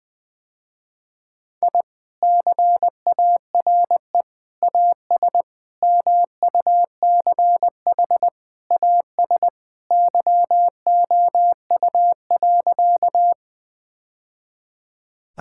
Morse code, heard as a keyed tone, about 21 wpm, 700 Hz. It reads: I CARE AS MUCH AS YOU.
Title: Sentences with 3-Word Phrases - Spoken Beginning - 20wpm